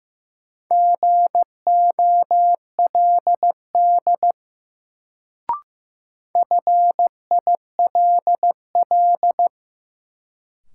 Morse code, heard as a keyed tone, about 15 wpm, 700 Hz. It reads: GOLD FILL
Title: Sentences from Top 1000 Words - Review Part 3 - New Word Drill - 15wpm